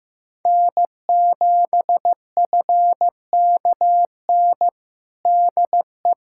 Morse code, heard as a keyed tone, about 15 wpm, 700 Hz. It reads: N7FKN DE
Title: Practice QSO - Exchange 4 - Ending - 15wpm